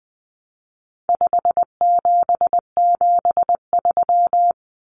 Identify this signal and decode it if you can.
Morse code, heard as a keyed tone, about 20 wpm, 700 Hz. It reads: 5773
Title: Common Cut Numbers - 4-Digit 20wpm